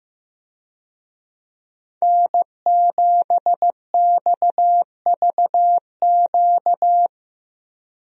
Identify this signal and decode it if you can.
Morse code, heard as a keyed tone, about 15 wpm, 700 Hz. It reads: N7XVQ
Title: USA Call Signs - 5 Characters - Repeated 2x - 15wpm